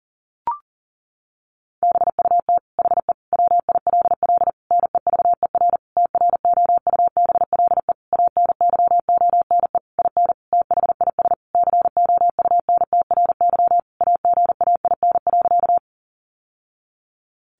Morse code, heard as a keyed tone, about 40 wpm, 700 Hz. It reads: BUT HE WILL NEVER TROUBLE ANYONE IN THIS COUNTRY AGAIN.
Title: Sentences from Top 1200 Words 40wpm